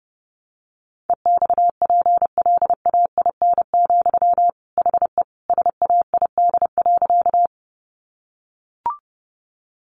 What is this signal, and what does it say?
Morse code, heard as a keyed tone, about 30 wpm, 700 Hz. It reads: EXPLAIN, HE SAID.
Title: Sentences from Top 1100 Words 30wpm